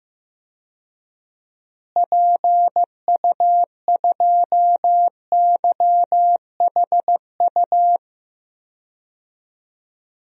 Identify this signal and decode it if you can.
Morse code, heard as a keyed tone, about 15 wpm, 700 Hz. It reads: PU2YHU
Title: DXCC-100 Call Signs - 6 Characters - Repeated 2x - 15wpm